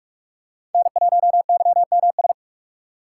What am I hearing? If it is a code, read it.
Morse code, heard as a keyed tone, about 43 wpm, 700 Hz. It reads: N1YMS
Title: USA Call Signs - 2x Character Spacing - 45wpm